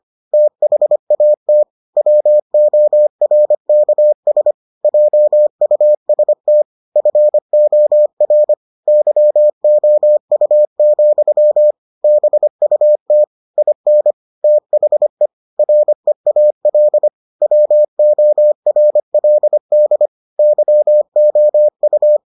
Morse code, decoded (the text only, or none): THAT WORKS JUST FOR YOU, BUT IN THE REAL WORLD YOU